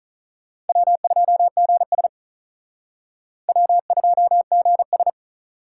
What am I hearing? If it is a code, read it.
Morse code, heard as a keyed tone, about 43 wpm, 700 Hz. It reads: W2GS W2GS
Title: Speed Racing - USA Call Signs 35wpm